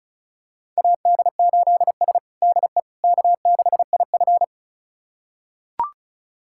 Morse code, heard as a keyed tone, about 36 wpm, 700 Hz. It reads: AD8S DE K6IF
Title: USA Callsign DE Callsign - 4 Characters - 35wpm